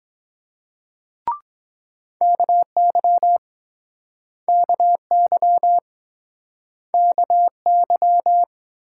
Morse code, heard as a keyed tone, about 24 wpm, 700 Hz. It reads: KY KY KY